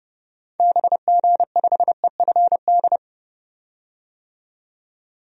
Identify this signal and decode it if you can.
Morse code, heard as a keyed tone, about 30 wpm, 700 Hz. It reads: BG5EFD
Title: DXCC-100 Call Signs - 6 Characters - 30wpm